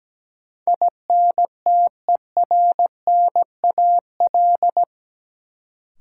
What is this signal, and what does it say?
Morse code, heard as a keyed tone, about 17 wpm, 700 Hz. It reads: INTERNAL
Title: Sentences from Top 1400 Words - New Word Drill - 17wpm